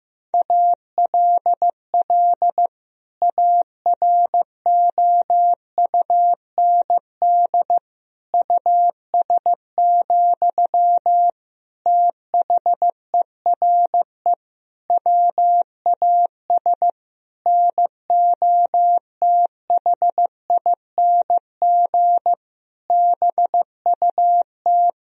Morse code, heard as a keyed tone, about 16 wpm, 700 Hz. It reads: ALL AROUND US, THERE WAS NOTHING BUT
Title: Sentences from Top 1000 Words - Review Part 3 - 15wpm